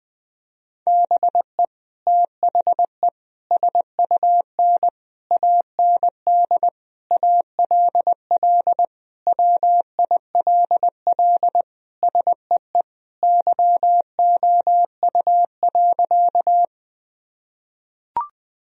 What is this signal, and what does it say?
Morse code, heard as a keyed tone, about 20 wpm, 700 Hz. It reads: BE THE SUN AND ALL WILL SEE YOU.
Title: Sentences from Top 1000 Words - Review Part 1 - 20wpm